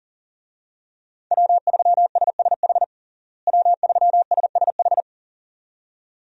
Morse code, heard as a keyed tone, about 40 wpm, 700 Hz. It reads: W3SSH W3SSH
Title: USA Call Signs - 5 Characters - Repeated 2x - 40wpm